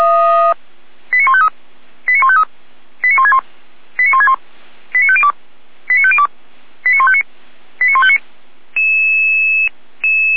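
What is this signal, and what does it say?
Five-tone selective call (ZVEI), heard as quick runs of five tones, tones of about 70 ms. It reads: 89153 89153 89161 89161 89502 89502 89159 89159
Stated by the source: ZVEI five-tone selective call 89099, received by SDR, from the Feuerwehr Orlamünde fire brigade audio server.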